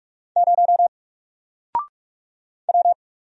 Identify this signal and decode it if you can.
Morse code, heard as a keyed tone, about 44 wpm, 700 Hz. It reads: 0 W
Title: Single Character (Letter:Number:Punctuation:Prosign) 45wpm